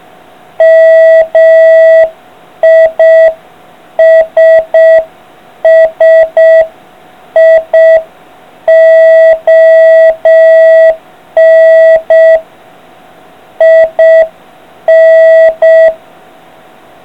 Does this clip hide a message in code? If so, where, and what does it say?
Yes: Morse code from 0.6 to 15.9 s, heard as a keyed tone, about 6 wpm, 650 Hz. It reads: MISSION IN